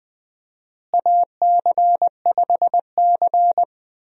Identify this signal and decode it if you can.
Morse code, heard as a keyed tone, about 20 wpm, 700 Hz. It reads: AC5C
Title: USA Call Signs - 4 Characters - Repeated 2x - 20wpm